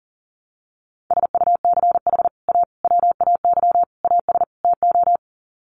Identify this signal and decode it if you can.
Morse code, heard as a keyed tone, about 40 wpm, 700 Hz. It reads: SUCH A WAY AS TO
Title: Sets of 5 Words Encore2 40wpm